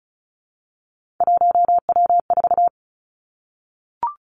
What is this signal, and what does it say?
Morse code, heard as a keyed tone, about 36 wpm, 700 Hz. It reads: 1W4
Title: Three Challenging Letter-Number Combinations 35wpm